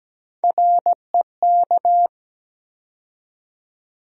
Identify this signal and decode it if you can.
Morse code, heard as a keyed tone, about 17 wpm, 700 Hz. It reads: REK
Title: USA Call Sign Postfixes - 3 Letters 17wpm